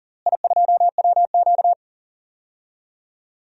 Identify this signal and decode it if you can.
Morse code, heard as a keyed tone, about 37 wpm, 700 Hz. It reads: I2WQ
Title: DXCC-100 Call Signs - 4 Characters - Repeated 3x - 40wpm